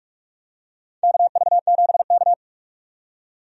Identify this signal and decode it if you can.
Morse code, heard as a keyed tone, about 45 wpm, 700 Hz. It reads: KV7K
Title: USA Call Signs - 4 Characters - 45wpm